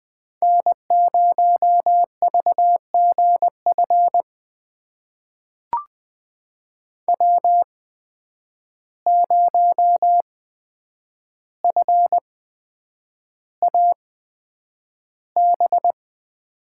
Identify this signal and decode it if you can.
Morse code, heard as a keyed tone, about 20 wpm, spaced as for 9 wpm, 700 Hz. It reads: N0VGF W0FAB
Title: USA Call Signs - 5 Characters - 8x Character Spacing - 20wpm